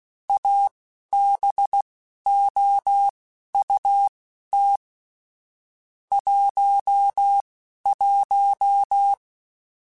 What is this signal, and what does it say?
Morse code, heard as a keyed tone, about 16 wpm, spaced as for 11 wpm, 800 Hz. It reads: ABOUT 11
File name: QSOs / QSOs-mp3s / WA7QOX 1@10